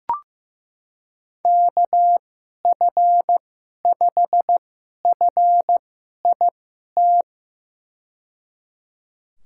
Morse code, heard as a keyed tone, about 15 wpm, spaced as for 11 wpm, 700 Hz. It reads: KF5FIT
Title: USA Call Signs - 6 Characters - 2x Character Spacing - 15wpm